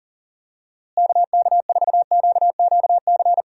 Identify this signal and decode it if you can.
Morse code, heard as a keyed tone, about 40 wpm, 700 Hz. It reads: KK4QQC
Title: USA Call Signs - 6 Characters - Repeated 3x - 40wpm